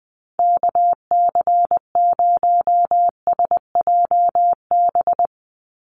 Morse code, heard as a keyed tone, about 19 wpm, 700 Hz. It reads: KC0SJB
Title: USA Call Signs 20wpm